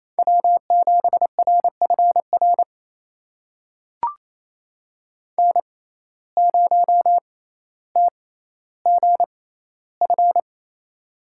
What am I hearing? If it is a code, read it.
Morse code, heard as a keyed tone, about 28 wpm, spaced as for 14 wpm, 700 Hz. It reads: W7RFR N0TGF
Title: USA Call Signs - 5 Characters - 6x Character Spacing - 28wpm